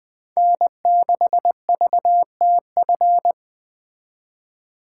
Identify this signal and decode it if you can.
Morse code, heard as a keyed tone, about 20 wpm, 700 Hz. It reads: N6VTF
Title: Speed Racing - USA 1x3 Call Signs 17wpm